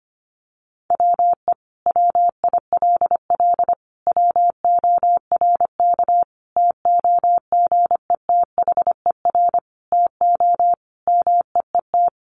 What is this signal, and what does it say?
Morse code, heard as a keyed tone, about 25 wpm, 700 Hz. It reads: WE WILL WORK TOGETHER TO MEET
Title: Sentences from Top 1100 Words 25wpm